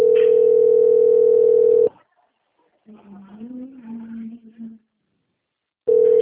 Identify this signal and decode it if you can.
Ringback tone (North America), 440 + 480 Hz.